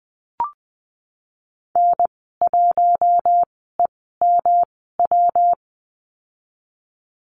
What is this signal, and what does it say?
Morse code, heard as a keyed tone, about 20 wpm, spaced as for 15 wpm, 700 Hz. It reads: N1EMW
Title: 154 Lesson 25g - Letter p - Call Signs 20wpm